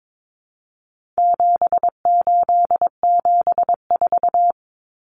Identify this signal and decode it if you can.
Morse code, heard as a keyed tone, about 22 wpm, 700 Hz. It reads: 7874
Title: Common Cut Numbers - 4-Digit 22wpm